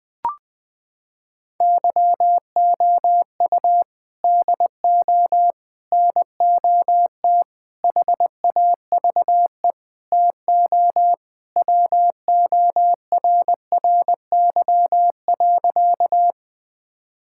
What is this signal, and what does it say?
Morse code, heard as a keyed tone, about 21 wpm, 700 Hz. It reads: YOU DO NOT HAVE TO WORRY.
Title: Sentences with 5-Word Phrases - 20wpm